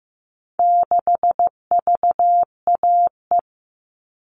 Morse code, heard as a keyed tone, about 15 wpm, 700 Hz. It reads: 6VAE
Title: All Cut Numbers - 4-Digit 15wpm